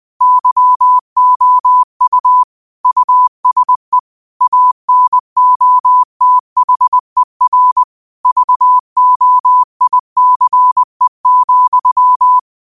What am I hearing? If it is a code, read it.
Morse code, heard as a keyed tone, about 20 wpm, 1000 Hz. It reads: YOU USE ANOTHER VOICE,
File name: qotdfiles / 20wpm